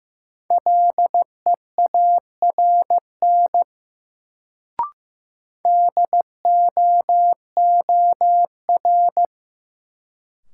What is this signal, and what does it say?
Morse code, heard as a keyed tone, about 15 wpm, 700 Hz. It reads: LEARN DOOR
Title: Sentences from Top 300 Words - New Word Drill - 15wpm